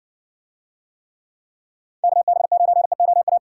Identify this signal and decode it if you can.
Morse code, heard as a keyed tone, about 62 wpm, 700 Hz. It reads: KB9EOA